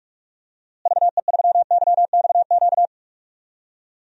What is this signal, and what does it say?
Morse code, heard as a keyed tone, about 46 wpm, 700 Hz. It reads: VE3YXQ
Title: DXCC-100 Call Signs - 6 Characters - Repeated 3x - 45wpm